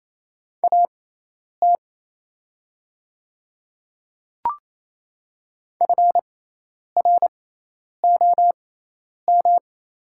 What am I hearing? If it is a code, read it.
Morse code, heard as a keyed tone, about 27 wpm, spaced as for 10 wpm, 700 Hz. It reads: AT FROM
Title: 6x Character Spacing - Top 100 Words - 28wpm